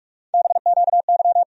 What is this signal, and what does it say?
Morse code, heard as a keyed tone, about 46 wpm, 700 Hz. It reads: BQY
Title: USA Call Sign Postfixes - 3 Letters 45wpm